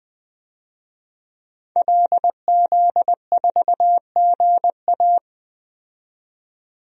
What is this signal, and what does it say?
Morse code, heard as a keyed tone, about 20 wpm, 700 Hz. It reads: LZ4GA
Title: DXCC-100 Call Signs - 5 Characters - 20wpm